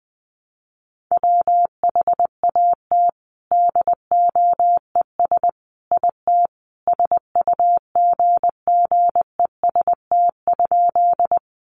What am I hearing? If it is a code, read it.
Morse code, heard as a keyed tone, about 20 wpm, 700 Hz. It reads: WHAT DOES IT SUGGEST?
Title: Sentences from Top 1300 Words 20wpm